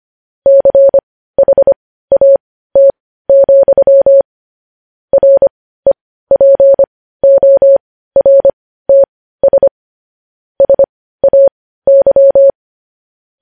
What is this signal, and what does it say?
Morse code, heard as a keyed tone, about 25 wpm, spaced as for 15 wpm, 550 Hz. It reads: CHAT, REPORTS SAY
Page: News headlines in morse code at 15 wpm